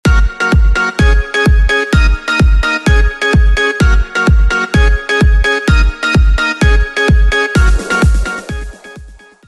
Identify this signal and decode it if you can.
There is no code.